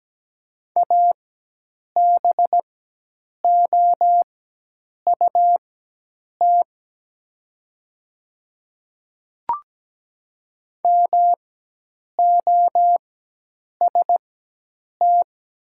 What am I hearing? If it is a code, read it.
Morse code, heard as a keyed tone, about 17 wpm, spaced as for 8 wpm, 700 Hz. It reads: ABOUT MOST
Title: 4x Character Spacing - Top 100 Words - 17wpm